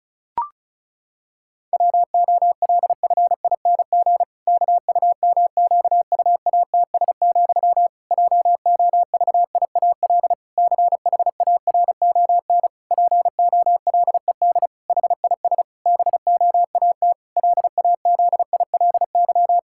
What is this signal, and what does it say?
Morse code, heard as a keyed tone, about 34 wpm, 700 Hz. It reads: WOLFING KUMQUATS, JOVIAL CHARON POLED HIS BOAT LAZILY
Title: Pangram Challenge v2 - 35wpm